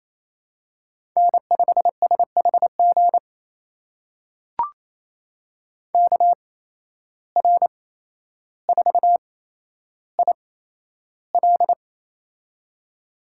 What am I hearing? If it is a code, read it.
Morse code, heard as a keyed tone, about 27 wpm, spaced as for 12 wpm, 700 Hz. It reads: N5SHG KR4IL
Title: USA Call Signs - 5 Characters - 8x Character Spacing - 28wpm